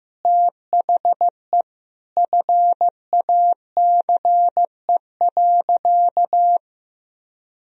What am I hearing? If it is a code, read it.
Morse code, heard as a keyed tone, about 16 wpm, 700 Hz. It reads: THE FACE.
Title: Sentences from Top 1000 Words - Review Part 2 - 15wpm